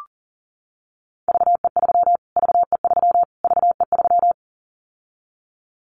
Morse code, heard as a keyed tone, about 40 wpm, 700 Hz. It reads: VE3 VE3 VE3
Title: Callsign Prefixes for Top 25 DXCC - Repeated 3x - 40wpm